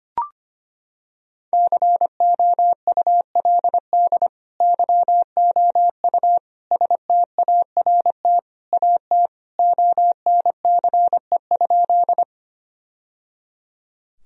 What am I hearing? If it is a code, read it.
Morse code, heard as a keyed tone, about 25 wpm, 700 Hz. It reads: COULD YOU START AT ONCE?
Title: Sentences from Top 1000 Words - Review Part 1 - 25wpm